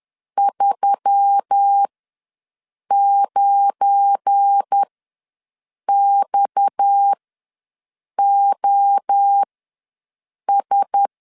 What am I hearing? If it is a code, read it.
Morse code, heard as a keyed tone, about 10 wpm, spaced as for 6 wpm, 800 Hz. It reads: 39XOS
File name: LEZIONI / 7 GF9Z / 7 GF9 ws 10wpm 4034